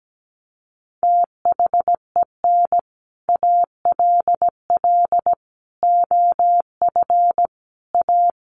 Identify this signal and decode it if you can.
Morse code, heard as a keyed tone, about 17 wpm, 700 Hz. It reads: THEN ALL OF A
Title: Sets of 5 Words Encore 17wpm